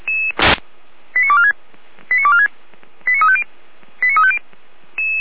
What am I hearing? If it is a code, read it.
Five-tone selective call (ZVEI), heard as quick runs of five tones, tones of about 70 ms. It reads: 89246 89246 89350 89350
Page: SDR/ZVEI